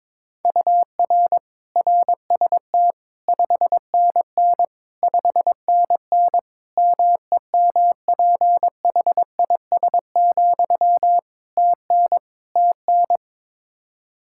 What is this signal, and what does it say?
Morse code, heard as a keyed tone, about 22 wpm, 700 Hz. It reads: UR RST 5NN 5NN MEMPHIS, TN TN
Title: Practice QSO - Exchange 1 - Beginning - 22wpm